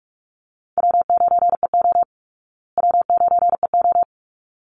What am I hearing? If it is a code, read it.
Morse code, heard as a keyed tone, about 44 wpm, 700 Hz. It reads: W9EO W9EO
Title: Easy Call Signs Repeated 2x - 45wpm